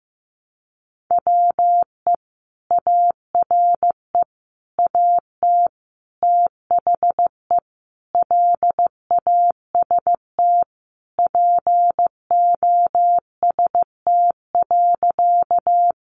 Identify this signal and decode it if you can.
Morse code, heard as a keyed tone, about 15 wpm, 700 Hz. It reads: WE ARE AT THE LAST POST.
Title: Sentences from Top 1000 Words 15wpm